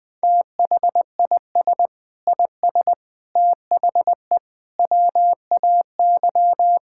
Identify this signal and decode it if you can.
Morse code, heard as a keyed tone, about 20 wpm, 700 Hz. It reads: THIS IS THE WAY
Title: Sentences with 4-Word Phrases - Spoken Beginning - 20wpm